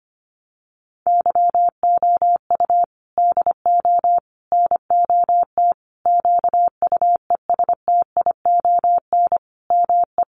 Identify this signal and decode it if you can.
Morse code, heard as a keyed tone, about 25 wpm, 700 Hz. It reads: YOU DO NOT QUESTION ME
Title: Sentences from Top 1100 Words 25wpm